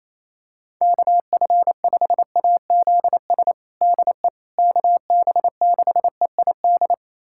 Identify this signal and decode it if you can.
Morse code, heard as a keyed tone, about 28 wpm, 700 Hz. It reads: KF5AZS DE KB6EID
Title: USA Callsign DE Callsign - 6 Characters - 28wpm